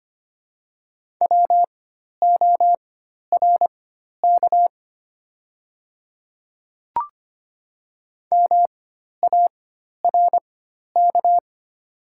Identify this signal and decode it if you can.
Morse code, heard as a keyed tone, about 25 wpm, spaced as for 12 wpm, 700 Hz. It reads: WORK MARK
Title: 4x Character Spacing - Top 300 Words - 25wpm